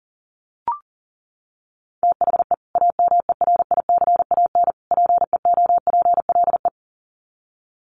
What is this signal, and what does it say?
Morse code, heard as a keyed tone, about 40 wpm, 700 Hz. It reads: THE AMERICAN PEOPLE
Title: Sets of 3 Words 40wpm